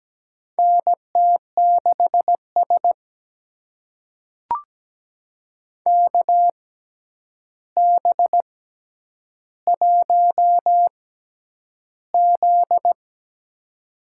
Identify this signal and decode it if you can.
Morse code, heard as a keyed tone, about 17 wpm, spaced as for 9 wpm, 700 Hz. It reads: NT6S KB1Z
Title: USA Call Signs - 4 Characters - 6x Character Spacing - 17wpm